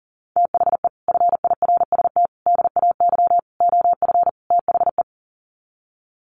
Morse code, heard as a keyed tone, about 40 wpm, 700 Hz. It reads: THE FIRST DAY OF THE
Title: Sets of 5 Words Encore2 40wpm